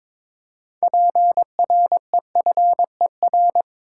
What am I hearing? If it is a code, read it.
Morse code, heard as a keyed tone, about 22 wpm, 700 Hz. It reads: PREFER
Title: Six Letter Words - 22wpm